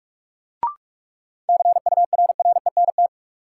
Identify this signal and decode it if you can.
Morse code, heard as a keyed tone, about 46 wpm, 700 Hz. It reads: CURRENT
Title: Sentences from Top 1600 Words - New Word Drill Alt - 45wpm